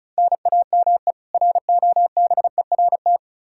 Morse code, heard as a keyed tone, about 34 wpm, 700 Hz. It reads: NAME ROBERT
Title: Practice QSO - Exchange 1 - Beginning - 35wpm